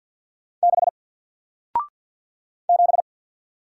Morse code, heard as a keyed tone, about 54 wpm, 700 Hz. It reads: 6 7
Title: Single Letter-Number Alt 50wpm